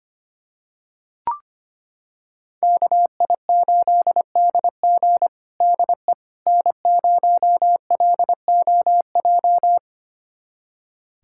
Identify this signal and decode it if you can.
Morse code, heard as a keyed tone, about 25 wpm, 700 Hz. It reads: KI8DG DE N0LOJ
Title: USA Callsign DE Callsign - 5 Characters - 25wpm